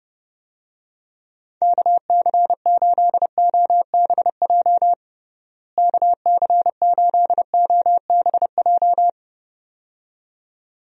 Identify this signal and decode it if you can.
Morse code, heard as a keyed tone, about 32 wpm, 700 Hz. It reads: KC8OBJ KC8OBJ
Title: USA Call Signs - 6 Characters - Repeated 2x - 30wpm